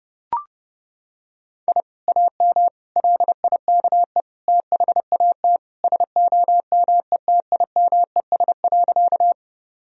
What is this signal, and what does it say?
Morse code, heard as a keyed tone, about 30 wpm, 700 Hz. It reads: I AM LIKE THAT SOMETIMES.
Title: Sentences from Top 1000 Words - Review Part 3 - 30wpm